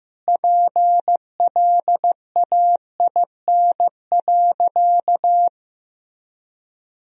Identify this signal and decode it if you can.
Morse code, heard as a keyed tone, about 15 wpm, 700 Hz. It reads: PLAIN.
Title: Sentences from Top 1000 Words - Review Part 2 - 15wpm